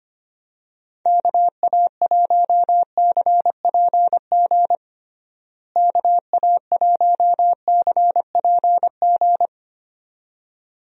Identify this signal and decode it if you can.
Morse code, heard as a keyed tone, about 25 wpm, 700 Hz. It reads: KA1CPG KA1CPG
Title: USA Call Signs - 6 Characters - Repeated 2x - 25wpm